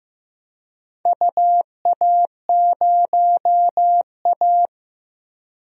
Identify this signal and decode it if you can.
Morse code, heard as a keyed tone, about 15 wpm, 700 Hz. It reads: UA0A
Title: DXCC-100 Call Signs - 4 Characters - 15wpm